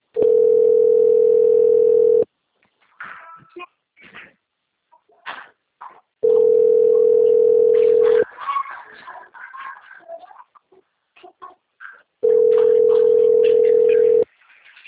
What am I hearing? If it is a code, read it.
Ringback tone (North America), 440 + 480 Hz.
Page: Dantas - PE